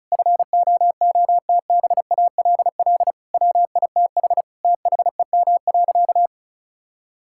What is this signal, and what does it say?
Morse code, heard as a keyed tone, about 36 wpm, 700 Hz. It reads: FOOTBALL WITH THEM.
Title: Sentences from Top 1600 Words 35wpm